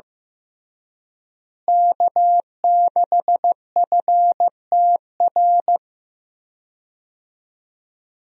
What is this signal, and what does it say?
Morse code, heard as a keyed tone, about 15 wpm, 700 Hz. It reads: K6FTR
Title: USA Call Signs - 5 Characters - Repeated 3x - 15wpm